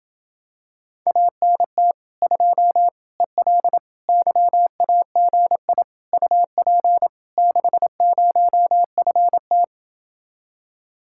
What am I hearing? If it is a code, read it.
Morse code, heard as a keyed tone, about 28 wpm, 700 Hz. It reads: ANT 2 EL YAGI UP 60FT